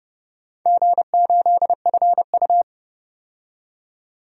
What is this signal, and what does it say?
Morse code, heard as a keyed tone, about 30 wpm, 700 Hz. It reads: G8FU
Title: DXCC-100 Call Signs - 4 Characters - 30wpm